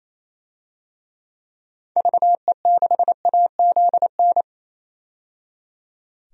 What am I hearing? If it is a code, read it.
Morse code, heard as a keyed tone, about 28 wpm, 700 Hz. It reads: VE6AZN
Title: DXCC-100 Call Signs - 6 Characters - 28wpm